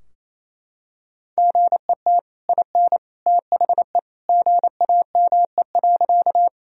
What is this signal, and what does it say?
Morse code, heard as a keyed tone, about 28 wpm, 700 Hz. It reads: GET IN THE GAME.
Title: Sentences from Top 1000 Words - Review Part 3 - 28wpm